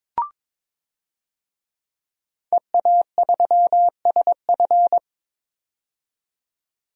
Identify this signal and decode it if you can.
Morse code, heard as a keyed tone, about 22 wpm, 700 Hz. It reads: EA3SF
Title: DXCC-100 Call Signs - 5 Characters - 22wpm